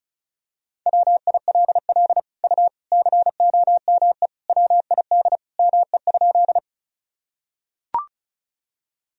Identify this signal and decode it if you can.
Morse code, heard as a keyed tone, about 36 wpm, 700 Hz. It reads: WILL U COME WID ME?
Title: Sentences from Top 200 Words and Abbreviations 35wpm